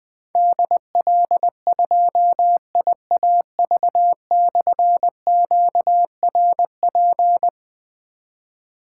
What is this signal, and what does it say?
Morse code, heard as a keyed tone, about 20 wpm, 700 Hz. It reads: DL2IAV/QRP
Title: Difficult Call Signs 20wpm